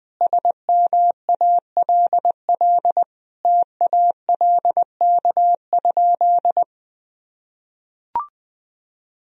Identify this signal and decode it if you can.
Morse code, heard as a keyed tone, about 20 wpm, 700 Hz. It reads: SMALL TALK?
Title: Sentences from Top 1000 Words - Review Part 1 - 20wpm